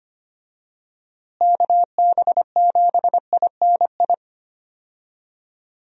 Morse code, heard as a keyed tone, about 25 wpm, 700 Hz. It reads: KB7INI
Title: USA Call Signs - 6 Characters - Repeated 2x - 25wpm